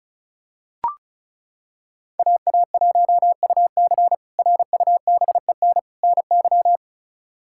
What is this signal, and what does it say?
Morse code, heard as a keyed tone, about 36 wpm, 700 Hz. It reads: AA1UC RUBEN NY
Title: K1USN SST QSO Practice - 35wpm